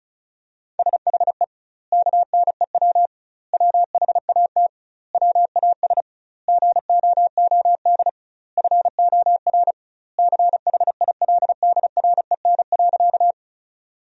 Morse code, heard as a keyed tone, about 35 wpm, 700 Hz. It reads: SHE KNEW WHAT WAS GOOD FOR CHILDREN.
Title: 2x Word-Spacing - Sentences from Top 400 Words - 35wpm